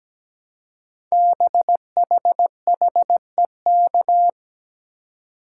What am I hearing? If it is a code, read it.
Morse code, heard as a keyed tone, about 17 wpm, 700 Hz. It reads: BHHEK
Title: Code Groups - 5 Letters - 17wpm